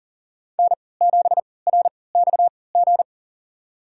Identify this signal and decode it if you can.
Morse code, heard as a keyed tone, about 40 wpm, spaced as for 23 wpm, 700 Hz. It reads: N7RXG
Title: USA Call Signs - 3x Character Spacing - 40wpm